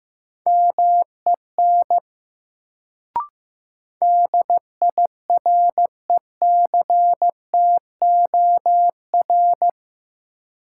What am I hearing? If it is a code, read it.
Morse code, heard as a keyed tone, about 15 wpm, 700 Hz. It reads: MEN DIRECTOR
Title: Sets of 3 Words - New Word Drill - 15wpm